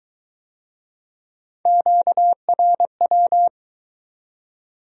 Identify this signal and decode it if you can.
Morse code, heard as a keyed tone, about 22 wpm, 700 Hz. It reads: QRW